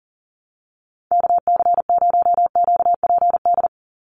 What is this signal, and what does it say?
Morse code, heard as a keyed tone, about 40 wpm, 700 Hz. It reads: KC0QPD
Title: USA Call Signs 40wpm